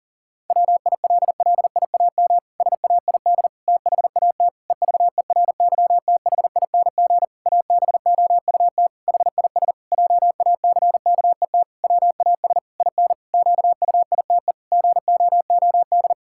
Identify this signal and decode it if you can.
Morse code, heard as a keyed tone, about 40 wpm, 700 Hz. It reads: WILLIAM SAID THAT EVERYTHING ABOUT HIS JACKET WAS IN QUITE GOOD
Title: Pangram Challenge v2 - 40wpm